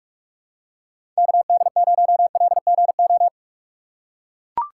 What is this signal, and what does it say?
Morse code, heard as a keyed tone, about 44 wpm, 700 Hz. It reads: KD0LGO
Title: USA Call Signs - 6 Characters - 6x Character Spacing - 45wpm